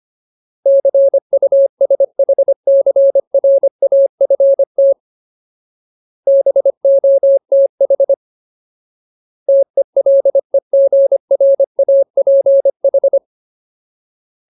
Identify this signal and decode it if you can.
Morse code, heard as a keyed tone, about 25 wpm, 550 Hz. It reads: CUSHCRAFT BOTH TELEGRAPH
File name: Ultimate Word List / 25WPM Ultimate